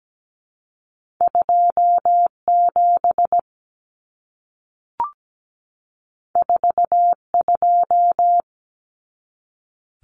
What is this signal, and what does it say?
Morse code, heard as a keyed tone, about 17 wpm, 700 Hz. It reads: 27 42
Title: Common Cut Numbers - 2-Digit 17wpm